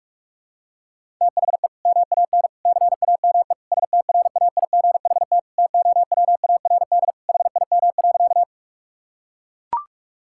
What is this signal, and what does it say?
Morse code, heard as a keyed tone, about 41 wpm, 700 Hz. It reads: THE MAN CAME STRAIGHT TOWARD HIM.
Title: Sentences from Top 1000 Words - Review Part 6 - 45wpm